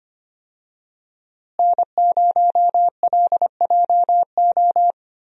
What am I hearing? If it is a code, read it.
Morse code, heard as a keyed tone, about 25 wpm, 700 Hz. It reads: N0LJO
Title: USA Call Signs - 5 Characters - Repeated 3x - 25wpm